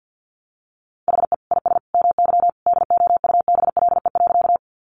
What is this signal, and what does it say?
Morse code, heard as a keyed tone, about 50 wpm, 700 Hz. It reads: HE IS MY DOUBLE.
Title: Sentences from Top 1400 Words 50wpm